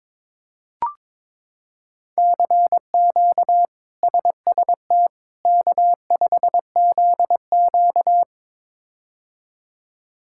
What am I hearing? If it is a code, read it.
Morse code, heard as a keyed tone, about 22 wpm, 700 Hz. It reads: CQ SST K5ZQ
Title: K1USN SST QSO Practice - 22wpm